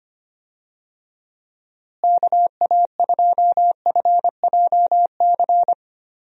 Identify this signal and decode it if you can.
Morse code, heard as a keyed tone, about 25 wpm, 700 Hz. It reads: KA2FJC
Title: USA Call Signs - 6 Characters - Repeated 2x - 25wpm